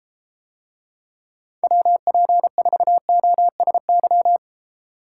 Morse code, heard as a keyed tone, about 33 wpm, 700 Hz. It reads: WP4OSY